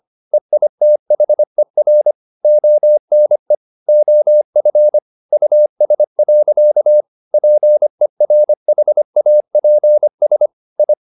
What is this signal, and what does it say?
Morse code, heard as a keyed tone, about 25 wpm, 600 Hz. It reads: EITHER ONE OF US. PERHAPS I
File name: the old man and the sea part 1 25wpm